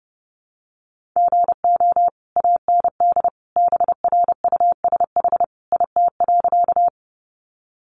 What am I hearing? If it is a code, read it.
Morse code, heard as a keyed tone, about 30 wpm, 700 Hz. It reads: GO AND BRUSH IT.
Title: Sentences from Top 1100 Words 30wpm